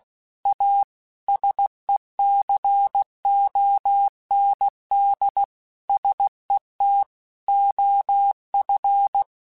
Morse code, heard as a keyed tone, about 16 wpm, 800 Hz. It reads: A SECOND SET OF
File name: Technical Subjects / Technical Subjects-mp3s / Sw Pwr Sup @15